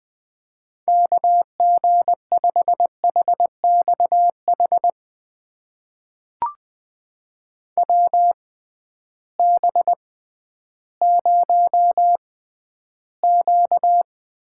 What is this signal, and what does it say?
Morse code, heard as a keyed tone, about 20 wpm, spaced as for 12 wpm, 700 Hz. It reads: KG5HXH WB0Q
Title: USA Call Signs - 6 Characters - 6x Character Spacing - 20wpm